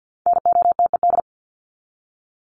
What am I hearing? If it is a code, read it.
Morse code, heard as a keyed tone, about 50 wpm, 700 Hz. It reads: NOTED
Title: ED Suffix - Top 100 Words with Base Words 50wpm